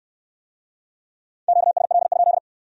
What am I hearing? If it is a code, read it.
Morse code, heard as a keyed tone, about 59 wpm, 700 Hz. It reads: QSL?